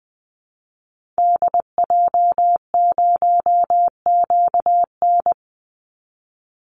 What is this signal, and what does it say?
Morse code, heard as a keyed tone, about 20 wpm, 700 Hz. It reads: DJ0QN
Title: Call Signs Repeated 3x - 20wpm